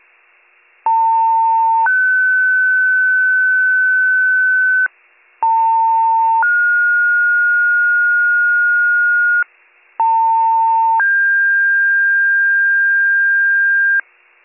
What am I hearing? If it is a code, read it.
Two-tone page, heard as a one-second tone, then a three-second tone, about 900 then 1500 Hz, about 900 then 1430 Hz, about 900 then 1600 Hz, with noise. It is heard from 0.9 to 14.0 s.